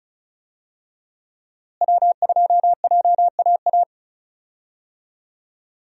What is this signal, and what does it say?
Morse code, heard as a keyed tone, about 36 wpm, 700 Hz. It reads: W2JAA
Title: USA Call Signs - 5 Characters - Repeated 2x - 35wpm